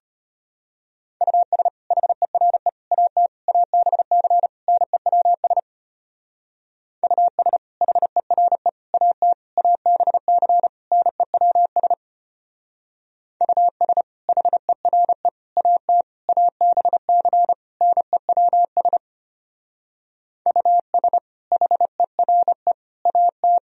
Morse code, heard as a keyed tone, about 32 wpm, 700 Hz. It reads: US HERE AT ABC NEWS US HERE AT ABC NEWS US HERE AT ABC NEWS US HERE AT